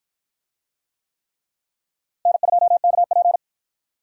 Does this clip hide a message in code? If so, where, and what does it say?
Yes: Morse code from 2.3 to 3.4 s, heard as a keyed tone, about 48 wpm, 700 Hz. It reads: N2KP